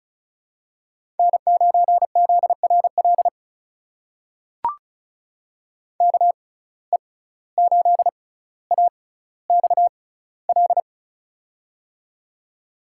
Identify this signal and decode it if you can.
Morse code, heard as a keyed tone, about 36 wpm, spaced as for 17 wpm, 700 Hz. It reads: N9ZRL KE8AXL
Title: USA Call Signs - 6x Character Spacing - 35wpm